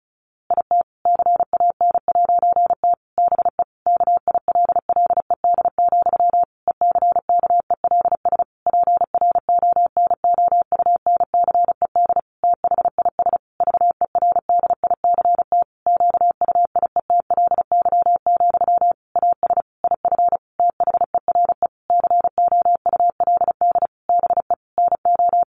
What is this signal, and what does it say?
Morse code, heard as a keyed tone, about 34 wpm, 700 Hz. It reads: IT CAN'T BE KILLED, ECKELS PRONOUNCED THIS VERDICT QUIETLY, AS IF THERE COULD BE NO
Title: A Sound Of Thunder 35wpm